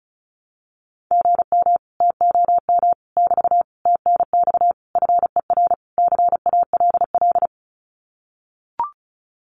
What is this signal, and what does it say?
Morse code, heard as a keyed tone, about 34 wpm, 700 Hz. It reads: GM TOM <BT> TNX FER CALL
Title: Common QSO Phrases Encore 35wpm